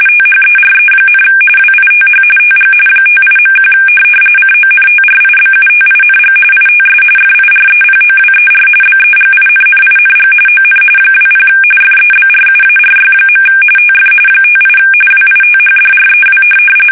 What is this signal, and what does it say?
RTTY, about 45.45 baud. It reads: THERE, WITH TG 58.1 CONTRIBUTING ITS FIREPOWER BEGINNING ON 20 FEBRUARY. TF 58 WITHDREW
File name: fsk850 / USS-Massachusetts